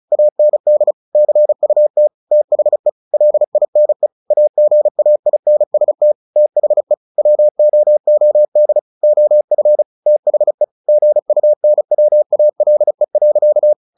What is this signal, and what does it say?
Morse code, heard as a keyed tone, about 34 wpm, 600 Hz. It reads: AND CUT THE LINE AGAINST THE WOOD OF THE GUNWALE.
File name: the old man and the sea / the old man and the sea 1 35wpm